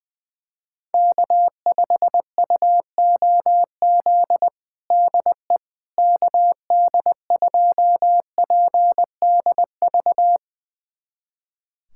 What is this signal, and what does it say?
Morse code, heard as a keyed tone, about 21 wpm, 700 Hz. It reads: K5UOZ DE KD2PDV
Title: USA Callsign DE Callsign 20wpm